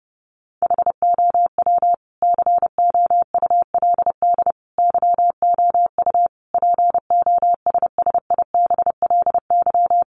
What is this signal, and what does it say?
Morse code, heard as a keyed tone, about 30 wpm, 700 Hz. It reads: HOW COULD YOU POSSIBLY
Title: Sentences from Top 1300 Words 30wpm